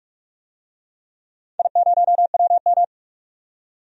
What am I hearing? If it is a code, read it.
Morse code, heard as a keyed tone, about 46 wpm, 700 Hz. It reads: I0WM
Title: DXCC-100 Call Signs - 4 Characters - 45wpm